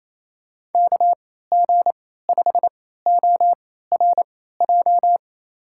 Morse code, heard as a keyed tone, about 28 wpm, spaced as for 16 wpm, 700 Hz. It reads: KG5ORJ
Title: USA Call Signs - 3x Character Spacing - 28wpm